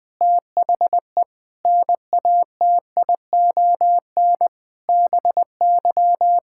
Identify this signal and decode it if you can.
Morse code, heard as a keyed tone, about 20 wpm, 700 Hz. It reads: THE NATION BY
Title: Sentences with 9-Word Phrases - 20wpm